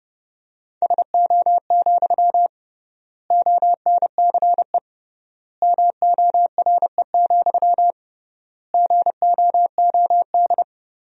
Morse code, heard as a keyed tone, about 32 wpm, 700 Hz. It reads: SO, ONCE MORE, GOOD
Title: 3x Word-Spacing - Sentences from Top 300 Words - 30wpm